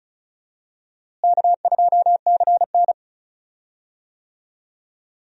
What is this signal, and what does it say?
Morse code, heard as a keyed tone, about 36 wpm, 700 Hz. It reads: K2CN
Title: USA Call Signs - 4 Characters - Repeated 2x - 35wpm